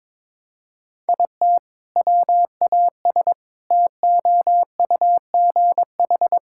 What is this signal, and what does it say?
Morse code, heard as a keyed tone, about 22 wpm, 700 Hz. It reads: IT WAS TOUGH
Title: Sentences from Top 1600 Words 22wpm